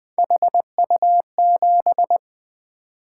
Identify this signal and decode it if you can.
Morse code, heard as a keyed tone, about 20 wpm, 700 Hz. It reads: HU7
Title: Three Letter-Number Combinations - 20wpm